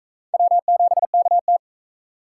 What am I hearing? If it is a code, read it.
Morse code, heard as a keyed tone, about 36 wpm, 700 Hz. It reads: W7KT